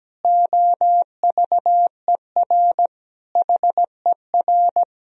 Morse code, heard as a keyed tone, about 17 wpm, 700 Hz. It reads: OVER HER
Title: Sentences from Top 1000 Words - Review Part 2 - 17wpm